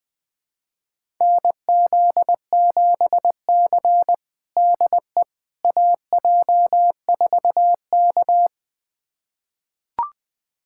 Morse code, heard as a keyed tone, about 20 wpm, 700 Hz. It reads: NZ7C DE AJ4K
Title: USA Callsign DE Callsign - 4 Characters - 20wpm